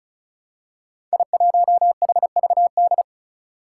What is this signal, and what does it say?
Morse code, heard as a keyed tone, about 36 wpm, 700 Hz. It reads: I1HVD
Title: Five Challenging Letter-Number Combinations 35wpm